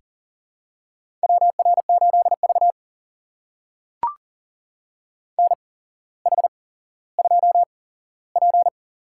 Morse code, heard as a keyed tone, about 40 wpm, spaced as for 17 wpm, 700 Hz. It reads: WR8V NH2P
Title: USA Call Signs - 4 Characters - 8x Character Spacing - 40wpm